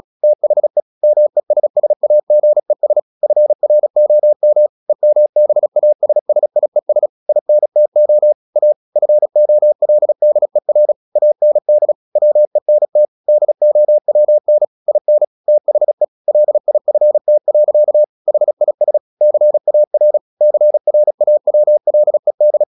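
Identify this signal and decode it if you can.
Morse code, heard as a keyed tone, about 36 wpm, 600 Hz. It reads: THE MESSAGES FROM EMBASSIES INTO A FOLDER AND WENT DOWN IN THE LIFT. HIS CAR CRAWLED
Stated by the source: Morse practice at 35 wpm from CW med Gnister, sending Ken Follett's novel Never in English.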